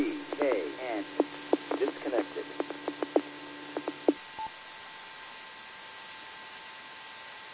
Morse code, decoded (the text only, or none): none